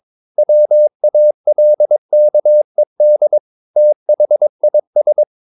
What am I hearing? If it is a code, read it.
Morse code, heard as a keyed tone, about 22 wpm, 600 Hz. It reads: WALKED THIS
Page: Romanen Never af Ken Follet på engelsk.